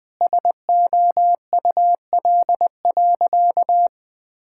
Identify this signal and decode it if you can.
Morse code, heard as a keyed tone, about 21 wpm, 700 Hz. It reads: SOUL.
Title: Sentences with 9-Word Phrases - 20wpm